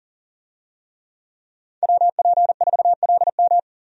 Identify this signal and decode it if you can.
Morse code, heard as a keyed tone, about 40 wpm, 700 Hz. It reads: WP4LM
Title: USA Call Signs - 5 Characters - 40wpm